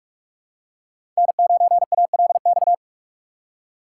Morse code, heard as a keyed tone, about 45 wpm, 700 Hz. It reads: N9ALX
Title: USA Call Signs - 5 Characters - 4x Character Spacing - 45wpm